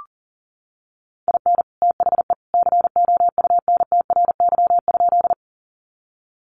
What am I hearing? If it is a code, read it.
Morse code, heard as a keyed tone, about 40 wpm, 700 Hz. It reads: IN THE COUNTRY?
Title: Sentences from Top 300 Words 40wpm